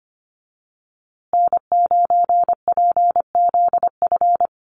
Morse code, heard as a keyed tone, about 25 wpm, 700 Hz. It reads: N9PZF
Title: USA Call Signs Repeated 3x 25wpm